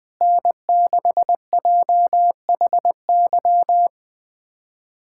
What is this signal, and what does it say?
Morse code, heard as a keyed tone, about 21 wpm, 700 Hz. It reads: N6JHY
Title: USA Callsign DE Callsign - 5 Characters - 20wpm